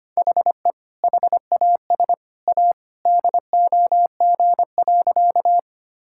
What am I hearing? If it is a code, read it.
Morse code, heard as a keyed tone, about 25 wpm, 700 Hz. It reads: HE HAS A DOG.
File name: practice / 25wpm / Sentences with 2-Word Phrases - 25wpm